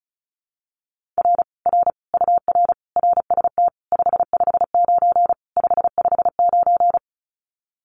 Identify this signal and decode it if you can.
Morse code, heard as a keyed tone, about 36 wpm, 700 Hz. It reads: R R UR RST 559 559
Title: Common QSO Phrases Encore 35wpm